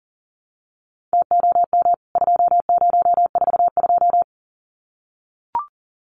Morse code, heard as a keyed tone, about 40 wpm, 700 Hz. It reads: TOM 2042
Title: CWT Exchange 40wpm